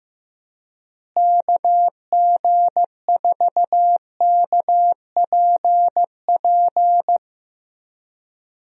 Morse code, heard as a keyed tone, about 15 wpm, 700 Hz. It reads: KG4KPP
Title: USA Call Signs - 6 Characters - Repeated 3x - 15wpm